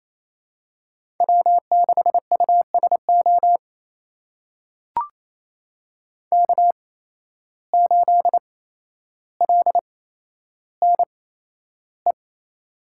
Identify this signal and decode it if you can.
Morse code, heard as a keyed tone, about 28 wpm, spaced as for 12 wpm, 700 Hz. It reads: W6USO K8LNE
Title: USA Call Signs - 5 Characters - 8x Character Spacing - 28wpm